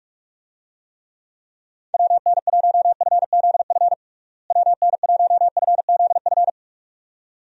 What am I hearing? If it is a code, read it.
Morse code, heard as a keyed tone, about 46 wpm, 700 Hz. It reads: WN1FZF WN1FZF
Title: USA Call Signs - 6 Characters - Repeated 2x - 45wpm